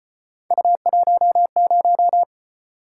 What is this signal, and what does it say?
Morse code, heard as a keyed tone, about 33 wpm, 700 Hz. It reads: U10